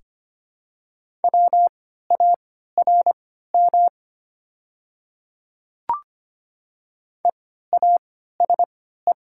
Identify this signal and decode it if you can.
Morse code, heard as a keyed tone, about 25 wpm, spaced as for 14 wpm, 700 Hz. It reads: WARM EASE
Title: 3x Character Spacing - Top 500 Words - 25wpm